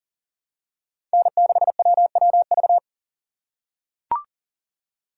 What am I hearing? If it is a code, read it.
Morse code, heard as a keyed tone, about 40 wpm, 700 Hz. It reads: N6WWV
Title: USA Call Signs - 5 Characters - 4x Character Spacing - 40wpm